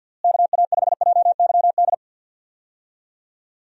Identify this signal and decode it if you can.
Morse code, heard as a keyed tone, about 52 wpm, 700 Hz. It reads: KA5JYD
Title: USA Call Signs - 6 Characters - Repeated 2x - 50wpm